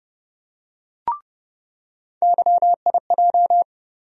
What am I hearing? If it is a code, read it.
Morse code, heard as a keyed tone, about 30 wpm, 700 Hz. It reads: YIJ
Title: USA Call Sign Postfixes - 3 Letters 30wpm